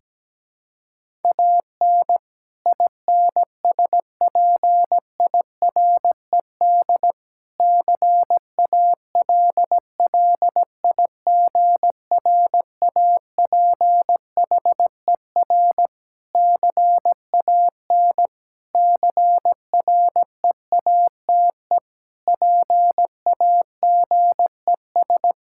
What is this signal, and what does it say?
Morse code, heard as a keyed tone, about 17 wpm, 700 Hz. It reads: AN INSPIRED CALLIGRAPHER CAN CREATE PAGES
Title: Pangram Challenge v2 - 17wpm